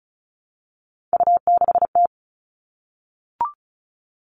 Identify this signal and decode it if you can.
Morse code, heard as a keyed tone, about 34 wpm, 700 Hz. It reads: U6T
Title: All Cut Numbers - 3-Digit 35wpm